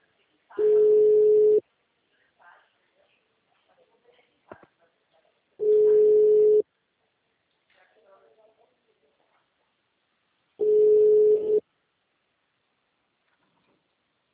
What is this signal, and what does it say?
Ringback tone (Europe, 425 Hz), with speech.
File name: out- -2005- - - .4370